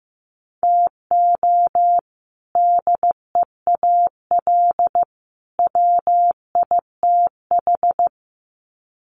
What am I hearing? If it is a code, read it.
Morse code, heard as a keyed tone, about 15 wpm, 700 Hz. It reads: TO DEAL WITH
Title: Sets of 3 Words 15wpm